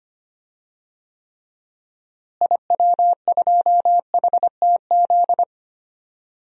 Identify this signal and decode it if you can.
Morse code, heard as a keyed tone, about 25 wpm, 700 Hz. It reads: IW2HTZ
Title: DXCC-100 Call Signs - 6 Characters - Repeated 3x - 25wpm